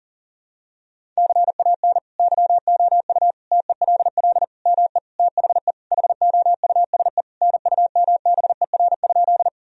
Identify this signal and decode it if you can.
Morse code, heard as a keyed tone, about 40 wpm, 700 Hz. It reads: CAN YOU TELL ME THE HOUSE NUMBER?
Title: Sentences from Top 1000 Words - Review Part 1 - 40wpm